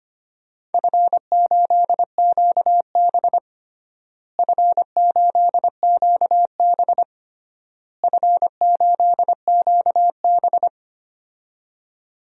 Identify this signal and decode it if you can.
Morse code, heard as a keyed tone, about 25 wpm, 700 Hz. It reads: F8QB F8QB F8QB
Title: DXCC-100 Call Signs - 4 Characters - Repeated 3x - 25wpm